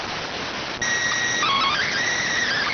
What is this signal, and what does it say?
SSTV (slow-scan television), PD 290.